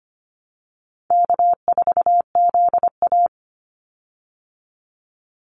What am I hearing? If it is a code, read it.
Morse code, heard as a keyed tone, about 25 wpm, 700 Hz. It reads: K4ZA
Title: Easy Call Signs Repeated 2x - 25wpm